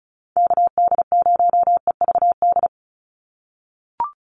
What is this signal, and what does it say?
Morse code, heard as a keyed tone, about 34 wpm, 700 Hz. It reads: KD0EVD
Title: USA Call Signs 35wpm